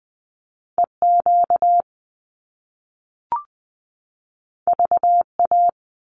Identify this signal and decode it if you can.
Morse code, heard as a keyed tone, about 20 wpm, 700 Hz. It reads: EQ VA
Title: Top Two Letter Combinations in Words 20wpm